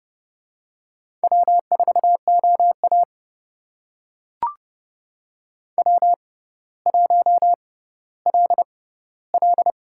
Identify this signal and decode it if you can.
Morse code, heard as a keyed tone, about 29 wpm, spaced as for 15 wpm, 700 Hz. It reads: W4OA W1LL
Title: USA Call Signs - 6x Character Spacing - 30wpm